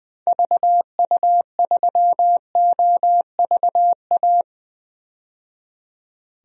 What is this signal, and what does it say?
Morse code, heard as a keyed tone, about 20 wpm, 700 Hz. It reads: VU3OVA
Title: DXCC-100 Call Signs - 6 Characters - 20wpm